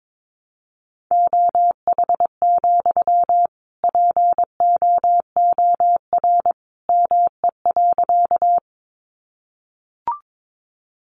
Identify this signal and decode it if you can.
Morse code, heard as a keyed tone, about 22 wpm, 700 Hz. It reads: OH, POOR ME.
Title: Sentences from Top 1000 Words 22wpm